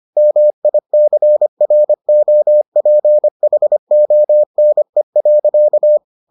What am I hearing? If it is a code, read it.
Morse code, heard as a keyed tone, about 25 wpm, 600 Hz. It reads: MICROPHONE.